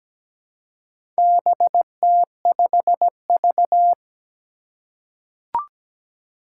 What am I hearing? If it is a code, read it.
Morse code, heard as a keyed tone, about 17 wpm, 700 Hz. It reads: BT5V
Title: Four Challenging Letter-Number Combinations 17wpm